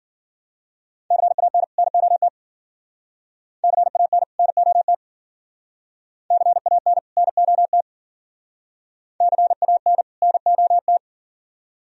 Morse code, heard as a keyed tone, about 51 wpm, 700 Hz. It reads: CAN NOT CAN NOT CAN NOT CAN NOT